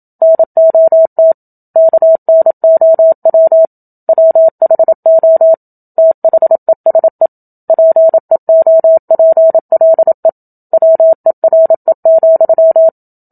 Morse code, heard as a keyed tone, about 27 wpm, 650 Hz. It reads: NOT KNOW WHO THESE PEOPLE WERE,
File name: Book Two / 01-Chapter 1 - 27 WPM